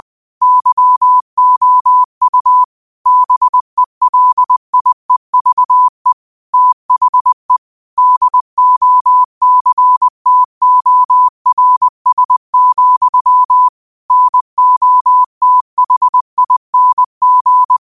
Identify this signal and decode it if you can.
Morse code, heard as a keyed tone, about 20 wpm, 1000 Hz. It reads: YOU BELIEVE THE DOCTORS, NOTHING
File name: qotdfiles / 20wpm